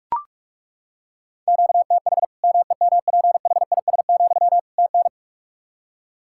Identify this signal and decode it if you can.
Morse code, heard as a keyed tone, about 41 wpm, 700 Hz. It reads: QTH MEMPHIS, TN
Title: Practice QSO - Exchange 1 - Beginning - 45wpm